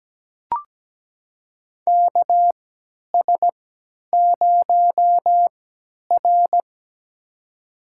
Morse code, heard as a keyed tone, about 17 wpm, spaced as for 10 wpm, 700 Hz. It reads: KS0R
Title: USA Call Signs - 4 Characters - 3x Character Spacing - 17wpm